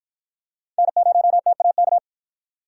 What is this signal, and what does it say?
Morse code, heard as a keyed tone, about 51 wpm, 700 Hz. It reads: N0TAK